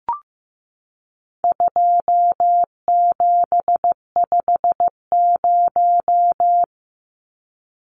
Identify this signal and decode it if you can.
Morse code, heard as a keyed tone, about 15 wpm, 700 Hz. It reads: 2750
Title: Numbers - 4-Digit 15wpm